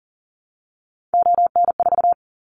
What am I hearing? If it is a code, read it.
Morse code, heard as a keyed tone, about 41 wpm, 700 Hz. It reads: ON4
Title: Callsign Prefixes for Top 50 DXCC - 40wpm